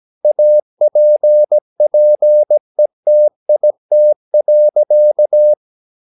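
Morse code, heard as a keyed tone, about 17 wpm, 600 Hz. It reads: APPETIT.
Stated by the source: Slow (QRS) Morse practice at 17 wpm, from CW med Gnister.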